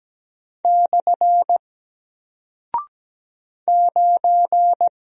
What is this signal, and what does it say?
Morse code, heard as a keyed tone, about 17 wpm, 700 Hz. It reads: / 9
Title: Single Character (Letter:Number:Punctuation:Prosign) Alt 17wpm